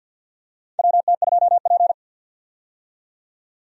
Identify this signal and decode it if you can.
Morse code, heard as a keyed tone, about 51 wpm, 700 Hz. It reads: WT2P
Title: USA Call Signs - 4 Characters - 50wpm